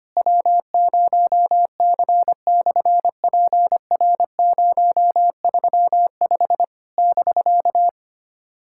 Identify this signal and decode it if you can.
Morse code, heard as a keyed tone, about 25 wpm, 700 Hz. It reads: W0C/PR035 <BK>
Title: SOTA QSO Practice 25wpm